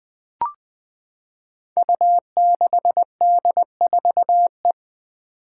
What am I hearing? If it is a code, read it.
Morse code, heard as a keyed tone, about 20 wpm, 700 Hz. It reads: U6D4E
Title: Five Challenging Letter-Number Combinations 20wpm